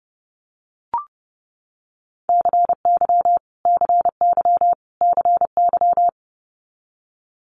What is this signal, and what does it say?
Morse code, heard as a keyed tone, about 30 wpm, 690 Hz. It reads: CY CY CY
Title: Multi-letter Phonemes Repeated 3x 30wpm